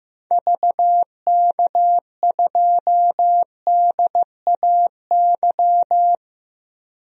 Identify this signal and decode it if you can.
Morse code, heard as a keyed tone, about 15 wpm, 700 Hz. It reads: VK2DAY
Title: DXCC-100 Call Signs - 6 Characters - Repeated 3x - 15wpm